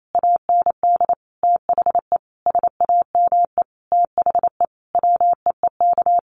Morse code, heard as a keyed tone, about 28 wpm, 700 Hz. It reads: AND THE SAME THE WEEK
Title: Sentences from Top 500 Words 28wpm